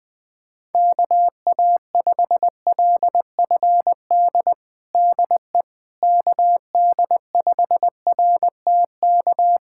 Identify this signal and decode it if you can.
Morse code, heard as a keyed tone, about 21 wpm, 700 Hz. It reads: KA5LFD DE KD5RTK
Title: USA Callsign DE Callsign - 6 Characters - 20wpm